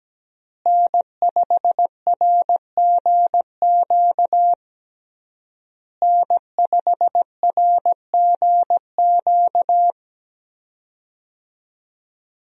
Morse code, heard as a keyed tone, about 17 wpm, 700 Hz. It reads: N5RGQ N5RGQ
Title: USA Call Signs - 5 Characters - Repeated 3x - 17wpm